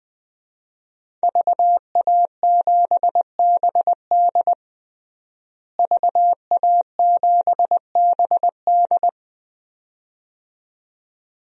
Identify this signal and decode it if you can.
Morse code, heard as a keyed tone, about 20 wpm, 700 Hz. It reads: VA7BD VA7BD
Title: DXCC-100 Call Signs - 5 Characters - Repeated 2x - 20wpm